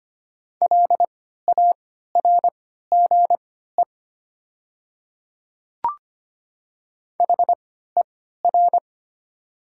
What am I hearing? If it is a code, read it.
Morse code, heard as a keyed tone, about 25 wpm, spaced as for 14 wpm, 700 Hz. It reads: LARGE HER
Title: 3x Character Spacing - Top 200 Words - 25wpm